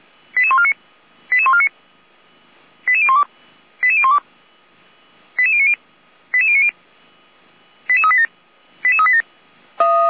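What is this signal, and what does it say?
Five-tone selective call (ZVEI), heard as quick runs of five tones, tones of about 80 ms. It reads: 88149 88149 88012 88012 88090 88090 88387 88387
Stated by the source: ZVEI five-tone selective call 88501 received on SDR, from the Feuerwehr Orlamünde audio archive.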